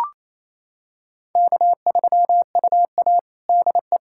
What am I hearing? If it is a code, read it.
Morse code, heard as a keyed tone, about 28 wpm, 700 Hz. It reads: K3UA DE
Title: Practice QSO - Exchange 4 - Ending - 28wpm